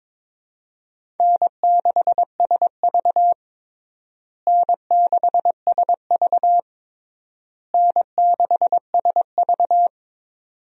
Morse code, heard as a keyed tone, about 22 wpm, 700 Hz. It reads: N6SV N6SV N6SV
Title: USA Call Signs - 4 Characters - Repeated 3x - 22wpm